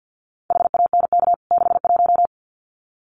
Morse code, heard as a keyed tone, about 50 wpm, 700 Hz. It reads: HANK 61
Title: CWT Exchange 50wpm